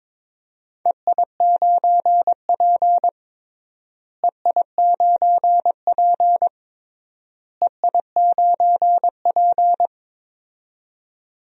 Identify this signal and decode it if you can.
Morse code, heard as a keyed tone, about 21 wpm, 700 Hz. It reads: EI9P EI9P EI9P
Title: DXCC-100 Call Signs - 4 Characters - Repeated 3x - 22wpm